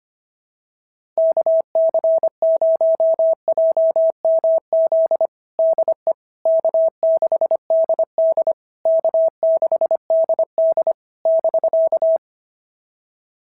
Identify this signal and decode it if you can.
Morse code, heard as a keyed tone, about 24 wpm, 650 Hz. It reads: KC0JMZ DE K6DD K6DD <BK>
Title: SOTA QSO Practice 25wpm